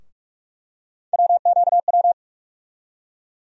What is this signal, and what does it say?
Morse code, heard as a keyed tone, about 46 wpm, 700 Hz. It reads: WQW
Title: USA Call Sign Postfixes - 3 Letters 45wpm